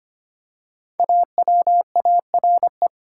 Morse code, heard as a keyed tone, about 25 wpm, 700 Hz. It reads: AWARE
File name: 5 Letter Words - First 3 Letters Given - 25wpm